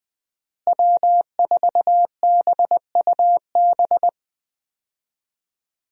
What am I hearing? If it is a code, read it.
Morse code, heard as a keyed tone, about 20 wpm, 700 Hz. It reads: W4BUB
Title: USA Call Signs - 5 Characters - Repeated 3x - 20wpm